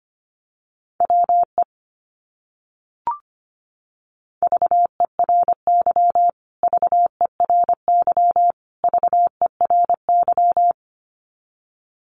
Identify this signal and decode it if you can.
Morse code, heard as a keyed tone, about 25 wpm, 700 Hz. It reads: WE VERY VERY VERY
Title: Top 200 Words Repeated 25wpm